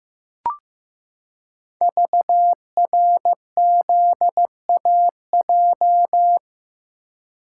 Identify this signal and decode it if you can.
Morse code, heard as a keyed tone, about 15 wpm, 700 Hz. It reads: VRZAJ
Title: Code Groups - 5 Letters - 15wpm